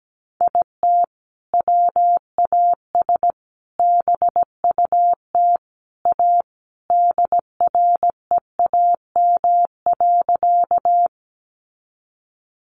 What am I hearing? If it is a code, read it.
Morse code, heard as a keyed tone, about 17 wpm, 700 Hz. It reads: IT WAS BUT A DREAM.
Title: Sentences from Top 1500 Words 17wpm